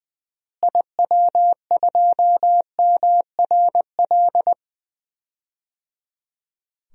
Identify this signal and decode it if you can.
Morse code, heard as a keyed tone, about 20 wpm, 700 Hz. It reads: IW2MRL
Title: DXCC-100 Call Signs - 6 Characters - Repeated 3x - 20wpm